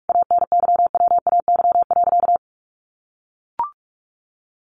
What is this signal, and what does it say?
Morse code, heard as a keyed tone, about 44 wpm, 700 Hz. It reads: ANYWAY.
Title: A Sound Of Thunder 45wpm